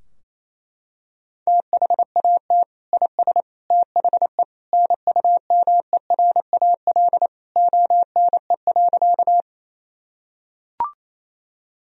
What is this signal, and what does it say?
Morse code, heard as a keyed tone, about 28 wpm, 700 Hz. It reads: THAT IS THE NUMERAL ONE.
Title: Sentences from Top 1000 Words - Review Part 2 - 28wpm